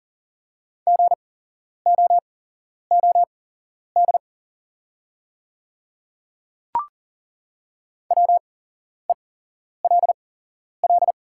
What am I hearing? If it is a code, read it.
Morse code, heard as a keyed tone, about 40 wpm, spaced as for 11 wpm, 690 Hz. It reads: GOOD WELL
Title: 8x Character Spacing - Top 200 Words - 40wpm